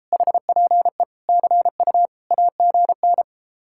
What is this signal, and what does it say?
Morse code, heard as a keyed tone, about 33 wpm, 700 Hz. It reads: HPE CU AGN